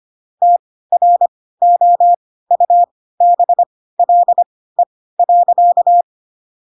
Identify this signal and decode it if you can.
Morse code, heard as a keyed tone, about 25 wpm, spaced as for 16 wpm, 700 Hz. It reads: TROUBLE.
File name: Pinocchio / Pinocchio ch3 10 25 ik1hge